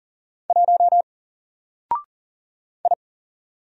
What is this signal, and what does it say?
Morse code, heard as a keyed tone, about 39 wpm, 700 Hz. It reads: 1 I
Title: Single Letter-Number 40wpm